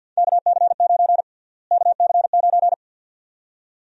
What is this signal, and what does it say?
Morse code, heard as a keyed tone, about 48 wpm, 700 Hz. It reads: KC9 KC9
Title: USA Call Sign Prefixes - Repeated 2x - 50wpm